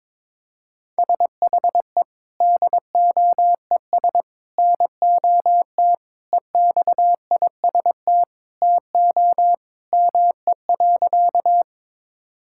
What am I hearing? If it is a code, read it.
Morse code, heard as a keyed tone, about 22 wpm, 700 Hz. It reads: SHE DOES NOT EXIST TO ME.
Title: Sentences from Top 1000 Words - Review Part 5 - 22wpm